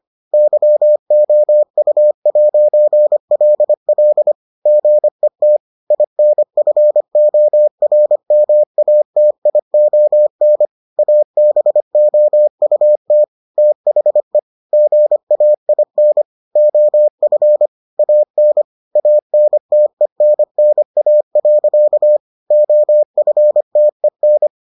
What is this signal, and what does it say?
Morse code, heard as a keyed tone, about 25 wpm, 600 Hz. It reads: YOU'LL GET INFORMATION ABOUT THE GAIN OF AN ANTENNA. OFTEN